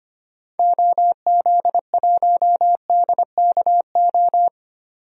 Morse code, heard as a keyed tone, about 25 wpm, 700 Hz. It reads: OZ1DKO
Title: DXCC-100 Call Signs - 6 Characters - 25wpm